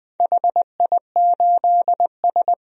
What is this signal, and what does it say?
Morse code, heard as a keyed tone, about 21 wpm, 700 Hz. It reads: HI8S
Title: DXCC-100 Call Signs - 4 Characters - Repeated 3x - 20wpm